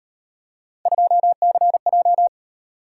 Morse code, heard as a keyed tone, about 38 wpm, 700 Hz. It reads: 2CJ